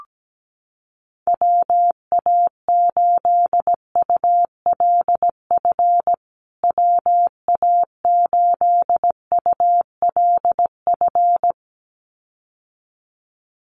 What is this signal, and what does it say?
Morse code, heard as a keyed tone, about 17 wpm, 700 Hz. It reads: WA8ULF WA8ULF
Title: USA Call Signs Repeated 2x 17wpm